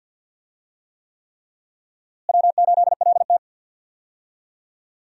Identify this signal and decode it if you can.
Morse code, heard as a keyed tone, about 46 wpm, 700 Hz. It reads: W8LT
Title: USA Call Signs - 4 Characters - Repeated 3x - 50wpm